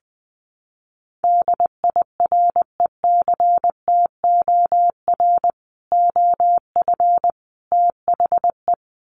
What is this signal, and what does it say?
Morse code, heard as a keyed tone, about 20 wpm, 700 Hz. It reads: DIRECTOR OF THE
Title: Sets of 3 Words 20wpm